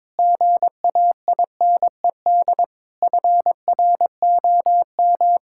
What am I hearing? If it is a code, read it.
Morse code, heard as a keyed tone, about 22 wpm, 700 Hz. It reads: GAINED FROM
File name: Difficult Sentences with Hints - 22wpm